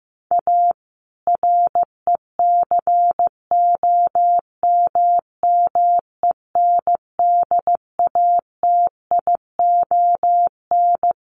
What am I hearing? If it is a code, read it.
Morse code, heard as a keyed tone, about 15 wpm, 700 Hz. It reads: A RECOMMENDATION
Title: Sentences from Top 1400 Words 15wpm